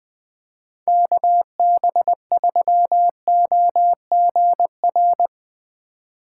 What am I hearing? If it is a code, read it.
Morse code, heard as a keyed tone, about 21 wpm, 700 Hz. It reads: KB3OGR